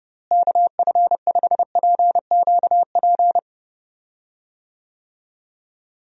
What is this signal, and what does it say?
Morse code, heard as a keyed tone, about 32 wpm, 700 Hz. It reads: KF5PQP
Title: USA Call Signs - 6 Characters - Repeated 2x - 30wpm